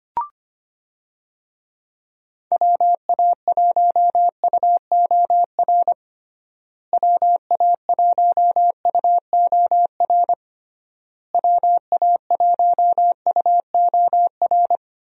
Morse code, heard as a keyed tone, about 25 wpm, 700 Hz. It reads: WA1UOR WA1UOR WA1UOR
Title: USA Call Signs - 6 Characters - Repeated 3x - 25wpm